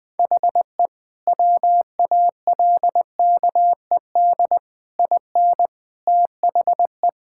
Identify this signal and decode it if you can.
Morse code, heard as a keyed tone, about 20 wpm, 700 Hz. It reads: HE WALKED IN THE
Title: Sentences with 5-Word Phrases - Spoken Beginning - 20wpm